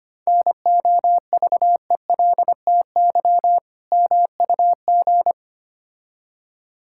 Morse code, heard as a keyed tone, about 25 wpm, 700 Hz. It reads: NOVELTY MUG
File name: practice / 25wpm / Difficult Words with Hints - 25wpm